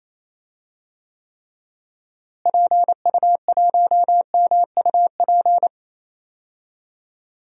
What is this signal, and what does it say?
Morse code, heard as a keyed tone, about 28 wpm, 700 Hz. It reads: PU1MUP
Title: DXCC-100 Call Signs - 6 Characters - Repeated 2x - 28wpm